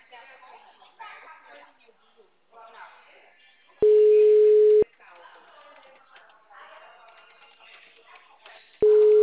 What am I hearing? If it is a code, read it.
Ringback tone (Europe, 425 Hz), with speech.